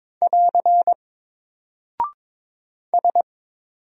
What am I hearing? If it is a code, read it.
Morse code, heard as a keyed tone, about 22 wpm, 700 Hz. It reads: <AR> S
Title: Single Character (Letter:Number:Punctuation:Prosign) Alt 22wpm